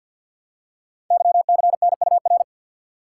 Morse code, heard as a keyed tone, about 51 wpm, 700 Hz. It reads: YCNUR
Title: Code Groups - 5 Letter:Numbers - 50wpm